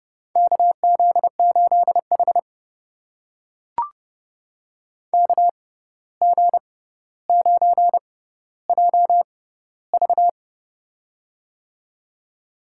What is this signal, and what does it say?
Morse code, heard as a keyed tone, about 30 wpm, spaced as for 14 wpm, 700 Hz. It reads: KZ8H KG9JV
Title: USA Call Signs - 6x Character Spacing - 30wpm